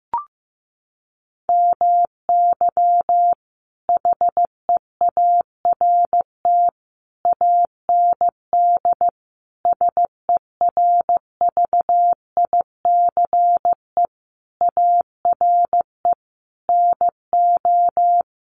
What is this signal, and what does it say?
Morse code, heard as a keyed tone, about 15 wpm, 700 Hz. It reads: MY HEART AND SERVICE ARE NO
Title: Sentences from Top 1200 Words 15wpm